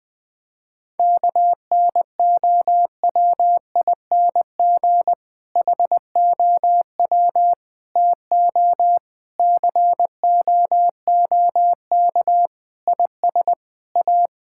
Morse code, heard as a keyed tone, about 20 wpm, 700 Hz. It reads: KNOWING HOW TO COOK IS A
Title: Difficult Sentences with Hints - 20wpm